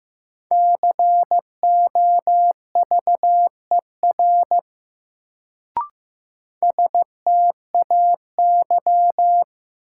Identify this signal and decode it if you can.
Morse code, heard as a keyed tone, about 15 wpm, 700 Hz. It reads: COVER STAY
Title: Sentences from Top 500 Words - New Word Drill - 15wpm